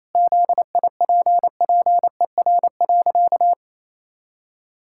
Morse code, heard as a keyed tone, about 28 wpm, 700 Hz. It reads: ZIPPER.
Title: Pangram Challenge v2 - 28wpm